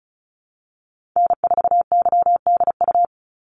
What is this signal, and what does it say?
Morse code, heard as a keyed tone, about 36 wpm, 700 Hz. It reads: N4YDU
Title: Call Signs - 35wpm